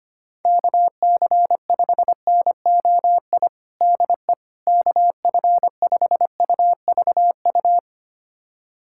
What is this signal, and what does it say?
Morse code, heard as a keyed tone, about 25 wpm, 700 Hz. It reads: KC5NOI DE KF5UVU
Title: USA Callsign DE Callsign - 6 Characters - 25wpm